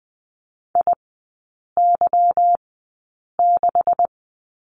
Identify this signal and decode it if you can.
Morse code, heard as a keyed tone, about 20 wpm, spaced as for 8 wpm, 700 Hz. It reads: IY6
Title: 212 Lesson 34e - Number 8 - Three Characters 20wpm